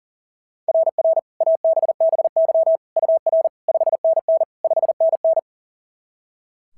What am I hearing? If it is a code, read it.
Morse code, heard as a keyed tone, about 37 wpm, 660 Hz. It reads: RR ABBY UR 5NN 5NN
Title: SOTA QSO Practice 40wpm